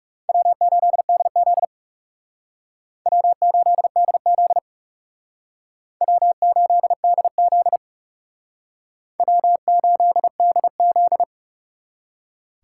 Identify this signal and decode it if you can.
Morse code, heard as a keyed tone, about 39 wpm, 690 Hz. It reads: W8DZ W8DZ W8DZ W8DZ